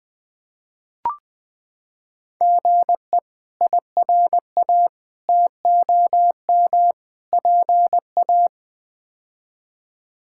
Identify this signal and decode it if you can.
Morse code, heard as a keyed tone, about 20 wpm, 700 Hz. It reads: GE IRA TOM PA
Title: K1USN SST QSO Practice - 20wpm